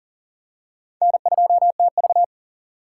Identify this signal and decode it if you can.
Morse code, heard as a keyed tone, about 37 wpm, 700 Hz. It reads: N2TV
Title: USA Call Signs - 4 Characters - 2x Character Spacing - 40wpm